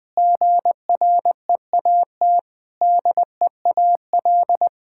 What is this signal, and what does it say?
Morse code, heard as a keyed tone, about 20 wpm, 700 Hz. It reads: GREAT DEAL
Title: Sentences with 6-Word Phrases - 20wpm